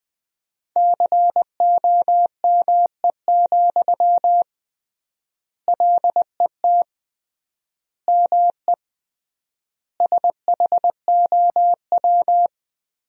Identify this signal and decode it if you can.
Morse code, heard as a keyed tone, about 21 wpm, 700 Hz. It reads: COME, LET ME SHOW
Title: 3x Word-Spacing - Sentences from Top 500 Words - 20wpm